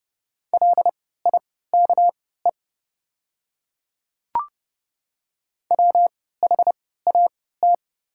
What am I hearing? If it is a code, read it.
Morse code, heard as a keyed tone, about 30 wpm, spaced as for 17 wpm, 700 Hz. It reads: LIKE WHAT
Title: 3x Character Spacing - Top 200 Words - 30wpm